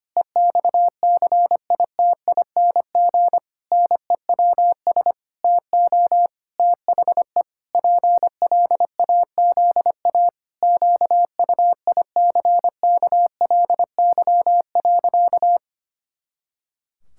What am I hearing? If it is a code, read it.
Morse code, heard as a keyed tone, about 25 wpm, 700 Hz. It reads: EXCITING NEWS TO THE PLAZA QUICKLY.
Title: Pangram Challenge v2 - 25wpm